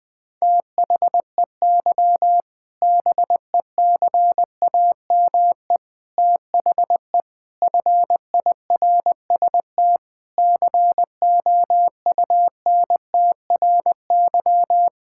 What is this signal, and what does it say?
Morse code, heard as a keyed tone, about 20 wpm, 700 Hz. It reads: THEY BECAME THE FIRST COUNTRY
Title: Sentences with 8-Word Phrases - Spoken Beginning - 20wpm